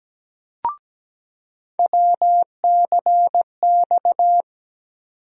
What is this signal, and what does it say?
Morse code, heard as a keyed tone, about 17 wpm, 700 Hz. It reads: WCX
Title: USA Call Sign Postfixes - 3 Letters 17wpm